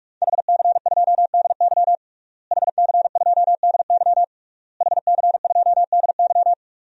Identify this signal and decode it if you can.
Morse code, heard as a keyed tone, about 46 wpm, 700 Hz. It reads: HC2DY HC2DY HC2DY
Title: DXCC-100 Call Signs - 5 Characters - Repeated 3x - 45wpm